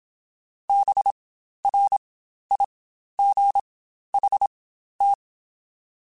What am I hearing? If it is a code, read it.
Morse code, heard as a keyed tone, about 26 wpm, spaced as for 12 wpm, 800 Hz. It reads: BRIGHT
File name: Spaced Characters / Spaced Characters-mp3s / Weather5 5 sp @25